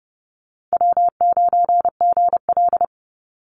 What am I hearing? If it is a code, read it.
Morse code, heard as a keyed tone, about 30 wpm, 700 Hz. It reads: W9GL
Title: Easy Call Signs - 30wpm